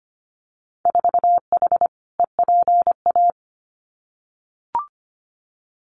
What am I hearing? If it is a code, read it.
Morse code, heard as a keyed tone, about 25 wpm, 700 Hz. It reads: 4H EPA
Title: Winter Field Day Exchanges 25wpm